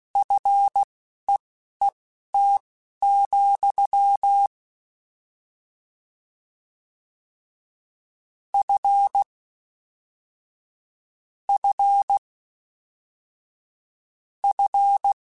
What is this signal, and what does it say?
Morse code, heard as a keyed tone, about 16 wpm, spaced as for 11 wpm, 800 Hz. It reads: FEET, F F F